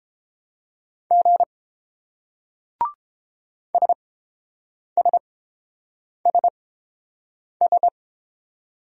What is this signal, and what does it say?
Morse code, heard as a keyed tone, about 29 wpm, spaced as for 7 wpm, 700 Hz. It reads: G SSSS